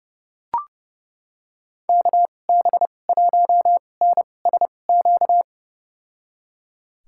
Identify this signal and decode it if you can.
Morse code, heard as a keyed tone, about 32 wpm, spaced as for 22 wpm, 700 Hz. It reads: KB1NSQ
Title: USA Call Signs - 6 Characters - 2x Character Spacing - 30wpm